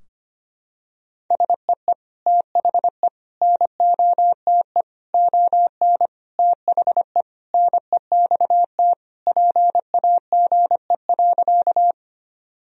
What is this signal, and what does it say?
Morse code, heard as a keyed tone, about 25 wpm, 700 Hz. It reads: SEE THE NOTE ON THE NEXT PAGE.
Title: Sentences from Top 1000 Words - Review Part 2 - 25wpm